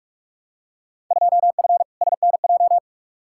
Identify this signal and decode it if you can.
Morse code, heard as a keyed tone, about 42 wpm, 700 Hz. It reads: 2F SNJ
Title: Field Day Exchanges Encore - 45wpm